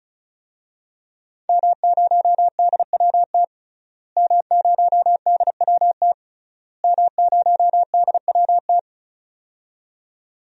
Morse code, heard as a keyed tone, about 36 wpm, 700 Hz. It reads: M0DWT M0DWT M0DWT
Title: DXCC-100 Call Signs - 5 Characters - Repeated 3x - 35wpm